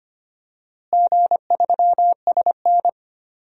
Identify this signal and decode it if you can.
Morse code, heard as a keyed tone, about 25 wpm, 700 Hz. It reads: G3SN
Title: DXCC-100 Call Signs - 4 Characters - Repeated 2x - 25wpm